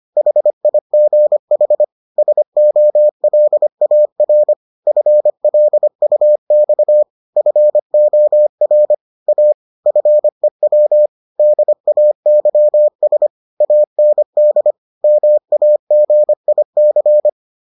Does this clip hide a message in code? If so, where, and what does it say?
Yes: Morse code from 0.2 to 17.3 s, heard as a keyed tone, about 25 wpm, 600 Hz. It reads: HIGH SOLAR FLUX FOR A FEW DAYS AND MAGIC